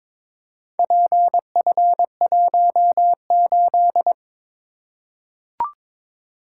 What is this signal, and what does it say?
Morse code, heard as a keyed tone, about 22 wpm, 700 Hz. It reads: PF18
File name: practice / 22wpm / Four Letter-Number Combinations - 22wpm